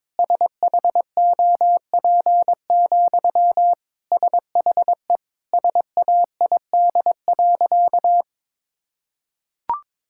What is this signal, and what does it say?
Morse code, heard as a keyed tone, about 22 wpm, 700 Hz. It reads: SHOP, SHE SAID.
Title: Sentences from Top 1600 Words 22wpm